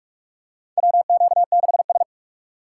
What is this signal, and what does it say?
Morse code, heard as a keyed tone, about 46 wpm, 700 Hz. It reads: WQ6S
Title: USA Call Signs - 4 Characters - 4x Character Spacing - 45wpm